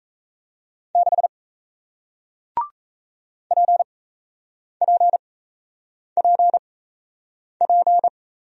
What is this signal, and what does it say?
Morse code, heard as a keyed tone, about 37 wpm, spaced as for 8 wpm, 700 Hz. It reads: 6 PPPP